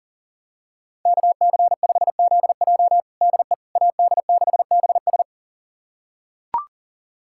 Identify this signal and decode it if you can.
Morse code, heard as a keyed tone, about 40 wpm, 700 Hz. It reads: KC5ZJ DE AD6BS
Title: USA Callsign DE Callsign - 5 Characters - 40wpm